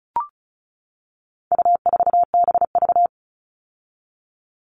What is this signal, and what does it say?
Morse code, heard as a keyed tone, about 34 wpm, 700 Hz. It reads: U4BV
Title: All Cut Numbers - 4-Digit 35wpm